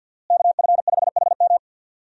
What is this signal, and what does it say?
Morse code, heard as a keyed tone, about 52 wpm, 700 Hz. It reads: KF5HM
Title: USA Call Signs - 5 Characters - Repeated 3x - 50wpm